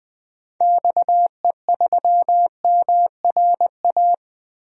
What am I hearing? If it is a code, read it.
Morse code, heard as a keyed tone, about 20 wpm, 700 Hz. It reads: XE3MRA
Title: DXCC-100 Call Signs - 6 Characters - 20wpm